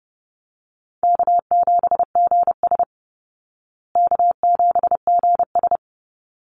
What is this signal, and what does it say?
Morse code, heard as a keyed tone, about 30 wpm, 700 Hz. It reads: K7GS K7GS
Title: Easy Call Signs Repeated 2x - 30wpm